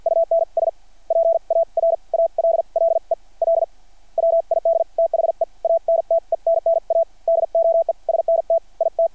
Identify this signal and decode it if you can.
Morse code, heard as a keyed tone, about 53 wpm, spaced as for 41 wpm, 660 Hz. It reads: UNS PARALLEL WID THE ANTENNA DOESNT IT
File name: archived / CRSnet